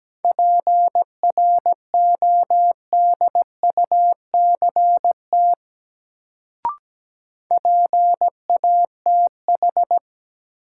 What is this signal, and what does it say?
Morse code, heard as a keyed tone, about 17 wpm, 700 Hz. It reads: PRODUCT PATH
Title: Sentences from Top 1400 Words - New Word Drill Alt - 17wpm